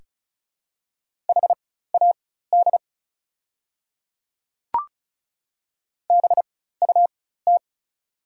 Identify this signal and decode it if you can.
Morse code, heard as a keyed tone, about 36 wpm, spaced as for 16 wpm, 700 Hz. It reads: HAD BUT
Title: 4x Character Spacing - Top 300 Words - 35wpm